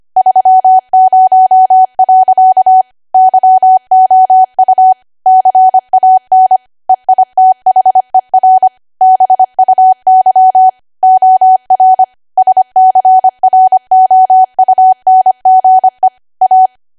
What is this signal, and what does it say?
Morse code, heard as a keyed tone, about 25 wpm, 750 Hz. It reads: 30. YOU CAN EITHER BUY OR SCROUNGE A
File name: cwa / W1AW-Level3 / 25WPM